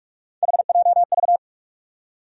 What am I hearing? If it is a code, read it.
Morse code, heard as a keyed tone, about 46 wpm, 700 Hz. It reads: HJV
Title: USA Call Sign Postfixes - 3 Letters 45wpm